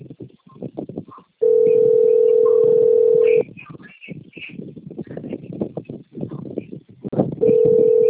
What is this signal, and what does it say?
Ringback tone (North America), 440 + 480 Hz.